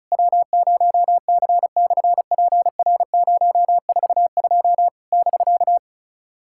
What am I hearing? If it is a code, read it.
Morse code, heard as a keyed tone, about 34 wpm, 700 Hz. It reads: W0C/PR042 <BK>
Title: SOTA QSO Practice 35wpm